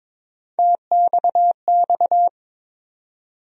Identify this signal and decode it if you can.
Morse code, heard as a keyed tone, about 22 wpm, 700 Hz. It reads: TXX
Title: USA Call Sign Postfixes - 3 Letters 22wpm